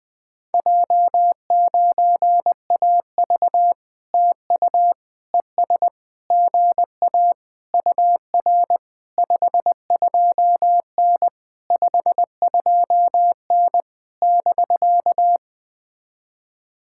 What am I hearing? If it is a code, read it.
Morse code, heard as a keyed tone, about 19 wpm, 700 Hz. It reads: J9AV TU ES GA UR 52N 52N <BK>
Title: POTA QSO Practice - 20wpm